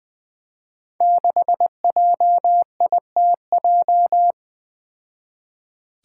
Morse code, heard as a keyed tone, about 20 wpm, 700 Hz. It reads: 6JITJ
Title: Five Challenging Letter-Number Combinations 20wpm